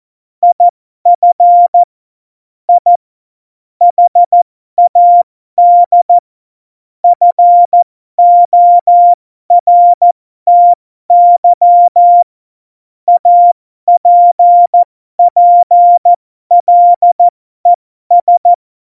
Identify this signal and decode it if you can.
Morse code, heard as a keyed tone, about 14 wpm, spaced as for 12 wpm, 700 Hz. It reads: IF I HAD FORTY APPLES
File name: Morsekurs / Witze